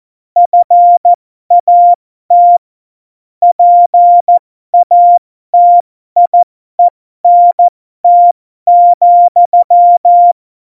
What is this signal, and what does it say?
Morse code, heard as a keyed tone, about 14 wpm, spaced as for 12 wpm, 700 Hz. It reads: FAT PATIENT,
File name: Morsekurs / Witze